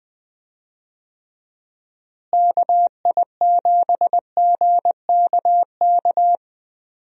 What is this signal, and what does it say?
Morse code, heard as a keyed tone, about 21 wpm, 700 Hz. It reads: KI7GKK
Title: USA Call Signs - 6 Characters - Repeated 3x - 20wpm